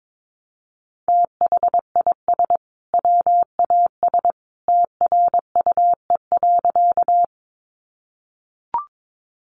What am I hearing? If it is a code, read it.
Morse code, heard as a keyed tone, about 22 wpm, 700 Hz. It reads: THIS WAS TRUE.
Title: Sentences from Top 400 Words 22wpm